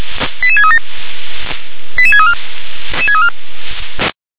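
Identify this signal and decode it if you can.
Five-tone selective call (ZVEI), heard as quick runs of five tones, tones of about 70 ms. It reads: 88637 88634
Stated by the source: ZVEI five-tone selective call received by SDR.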